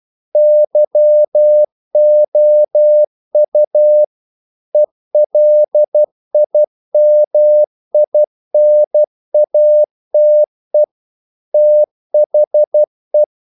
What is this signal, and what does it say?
Morse code, heard as a keyed tone, about 12 wpm, 600 Hz. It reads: YOU ELIMINATE THE
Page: English quotations in CW (telegraphy - morse)